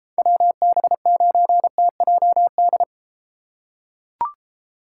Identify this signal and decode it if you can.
Morse code, heard as a keyed tone, about 30 wpm, 700 Hz. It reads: WB9TJD